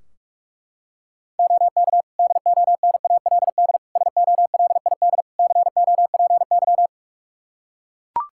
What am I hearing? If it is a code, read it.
Morse code, heard as a keyed tone, about 41 wpm, 700 Hz. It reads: OK DONALD SOLID COPY
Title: Practice QSO - Exchange 3 - More Info - 45wpm